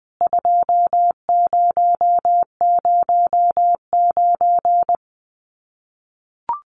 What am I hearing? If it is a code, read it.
Morse code, heard as a keyed tone, about 20 wpm, 700 Hz. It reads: 2009
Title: CWT Exchange 20wpm